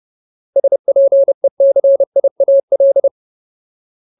Morse code, heard as a keyed tone, about 30 wpm, 550 Hz. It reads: SPECIAL
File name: Ultimate Word List / 30WPM Ultimate